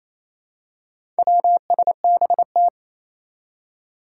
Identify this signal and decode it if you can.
Morse code, heard as a keyed tone, about 28 wpm, 700 Hz. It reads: WSBT
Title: Four Challenging Letter-Number Combinations 28wpm